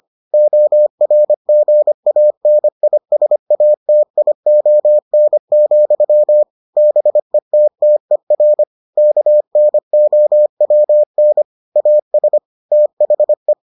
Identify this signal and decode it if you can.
Morse code, heard as a keyed tone, about 25 wpm, 600 Hz. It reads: ORGANISATION, BETTER KNOWN AS THE